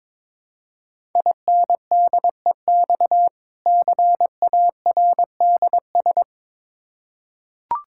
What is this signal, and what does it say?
Morse code, heard as a keyed tone, about 22 wpm, 690 Hz. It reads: INDEX CARDS
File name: Difficult Words with Hints - 22wpm